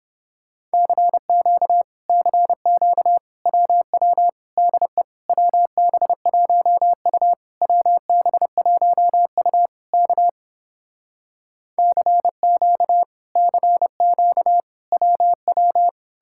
Morse code, heard as a keyed tone, about 31 wpm, 700 Hz. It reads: CQ CQ WW DE WB1U WB1U K CQ CQ WW